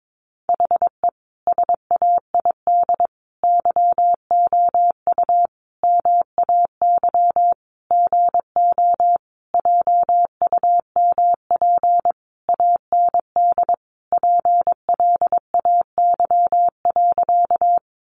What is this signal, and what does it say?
Morse code, heard as a keyed tone, about 22 wpm, 700 Hz. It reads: HE SAID YOU MAY GO JUMP AND PLAY.
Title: Sentences from Top 1300 Words 22wpm